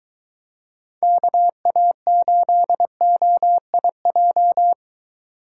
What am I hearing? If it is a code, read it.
Morse code, heard as a keyed tone, about 23 wpm, 700 Hz. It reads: KA8OIJ